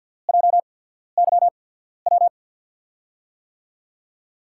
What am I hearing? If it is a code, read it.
Morse code, heard as a keyed tone, about 52 wpm, spaced as for 14 wpm, 700 Hz. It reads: JYW
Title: USA Call Signs - 6 Characters - 8x Character Spacing - 50wpm